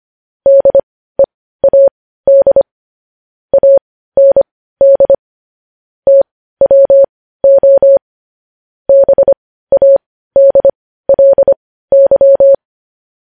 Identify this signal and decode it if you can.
Morse code, heard as a keyed tone, about 25 wpm, spaced as for 15 wpm, 550 Hz. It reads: DEAD AND TWO BADLY
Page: news-headlines-in-morse-code-at-15-wpm